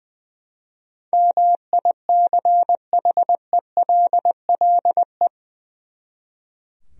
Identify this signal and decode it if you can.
Morse code, heard as a keyed tone, about 20 wpm, 700 Hz. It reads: MICHELLE
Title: USA Common Names - 20wpm